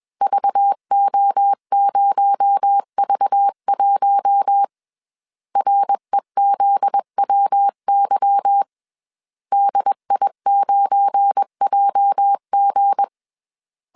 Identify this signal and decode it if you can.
Morse code, heard as a keyed tone, about 21 wpm, 800 Hz. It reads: VO0V1 REZWY DI9JG
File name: CW / DET / 1 9 1500C S C 20wpm 2310